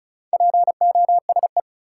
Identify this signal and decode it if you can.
Morse code, heard as a keyed tone, about 36 wpm, 700 Hz. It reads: POSE
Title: Sentences from Top 1000 Words - Review Part 2 - New Word Drill - 35wpm